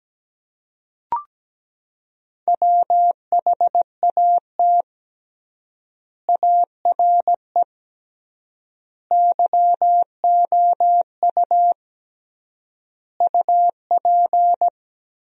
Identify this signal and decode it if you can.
Morse code, heard as a keyed tone, about 17 wpm, 700 Hz. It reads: WHAT ARE YOU UP
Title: 3x Word-Spacing - Sentences from Top 100 Words - 17wpm